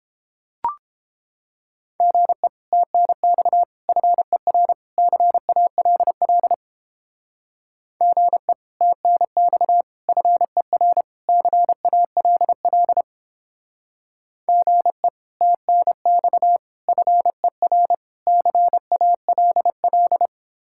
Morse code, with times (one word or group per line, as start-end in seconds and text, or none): 2.0-2.5 GE
2.7-3.6 TNX
3.9-4.7 FER
5.0-6.5 CALL
8.0-8.5 GE
8.8-9.8 TNX
10.1-11.0 FER
11.3-13.0 CALL
14.5-15.1 GE
15.4-16.6 TNX
16.9-17.9 FER
18.3-20.3 CALL